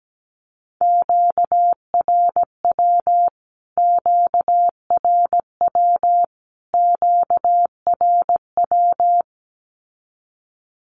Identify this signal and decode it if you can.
Morse code, heard as a keyed tone, about 17 wpm, 700 Hz. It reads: QRW QRW QRW
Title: Q-Codes Repeated 3x 17wpm